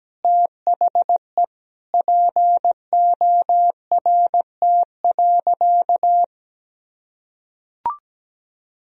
Morse code, heard as a keyed tone, about 17 wpm, 700 Hz. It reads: THE PORT.
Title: Sentences from Top 1000 Words - Review Part 2 - 17wpm